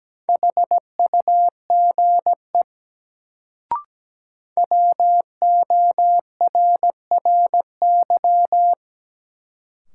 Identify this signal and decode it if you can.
Morse code, heard as a keyed tone, about 17 wpm, 700 Hz. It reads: HUGE WORRY
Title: Sentences from Top 1100 Words - New Word Drill - 17wpm